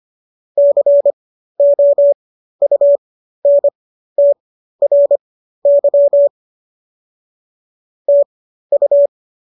Morse code, heard as a keyed tone, about 25 wpm, spaced as for 13 wpm, 570 Hz. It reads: COUNTRY TU